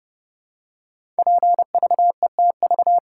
Morse code, heard as a keyed tone, about 30 wpm, 700 Hz. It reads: PVETV
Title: Code Groups - 5 Letters - 30wpm